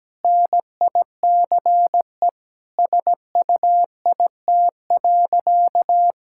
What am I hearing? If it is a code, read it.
Morse code, heard as a keyed tone, about 17 wpm, 700 Hz. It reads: NICE SUIT.
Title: Sentences from Top 1600 Words 17wpm